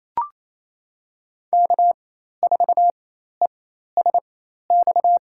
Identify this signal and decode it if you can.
Morse code, heard as a keyed tone, about 28 wpm, spaced as for 13 wpm, 700 Hz. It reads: K4ESX
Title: USA Call Signs - 5 Characters - 4x Character Spacing - 28wpm